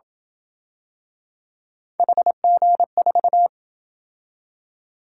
Morse code, heard as a keyed tone, about 27 wpm, 700 Hz. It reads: HG4